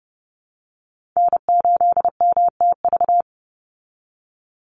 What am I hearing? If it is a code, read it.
Morse code, heard as a keyed tone, about 30 wpm, 700 Hz. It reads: N8MTV
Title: Call Signs Repeated 3x - 30wpm